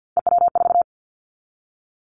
Morse code, heard as a keyed tone, about 50 wpm, 700 Hz. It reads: EW4
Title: Callsign Prefixes for Top 50 DXCC - 50wpm